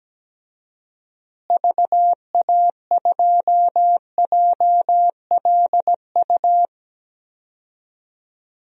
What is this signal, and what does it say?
Morse code, heard as a keyed tone, about 17 wpm, 700 Hz. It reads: VA2JLU
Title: DXCC-100 Call Signs - 6 Characters - Repeated 3x - 17wpm